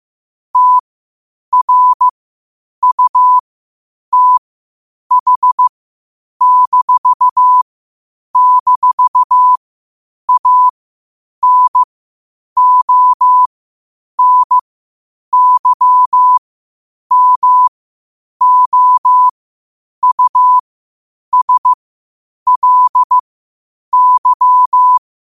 Morse code, heard as a keyed tone, about 15 wpm, spaced as for 8 wpm, 1000 Hz. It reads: TRUTH--ANONYMOUSLY
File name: qotdfiles / 8wpm